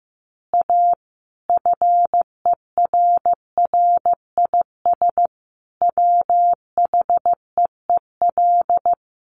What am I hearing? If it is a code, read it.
Morse code, heard as a keyed tone, about 15 wpm, 700 Hz. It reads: A FERRIS WHEEL
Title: Additional Easy Phrases 15wpm